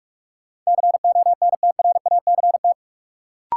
Morse code, heard as a keyed tone, about 46 wpm, 700 Hz. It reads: CONTRACT
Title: Sentences from Top 1400 Words - New Word Drill Alt - 45wpm